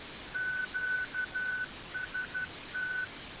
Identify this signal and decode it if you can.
Morse code, heard as a keyed tone, about 12 wpm, 1500 Hz. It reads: QST
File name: SwELNets / QSO 2020 09 18